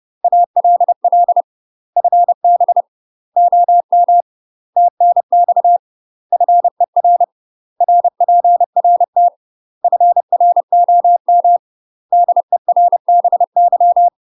Morse code, heard as a keyed tone, about 30 wpm, 700 Hz. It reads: ALL FB OM TNX FER RPRT FROM DERBY
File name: UK MASTER QSOs TRAINING / 30WPM / UK MASTER QSOs 0034